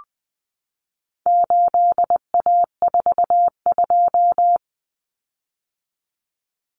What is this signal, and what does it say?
Morse code, heard as a keyed tone, about 20 wpm, 700 Hz. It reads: 8A42
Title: Common Cut Numbers - 4-Digit 20wpm